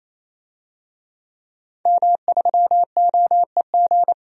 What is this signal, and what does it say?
Morse code, heard as a keyed tone, about 28 wpm, 700 Hz. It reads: M3OEG
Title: DXCC-100 Call Signs - 5 Characters - 28wpm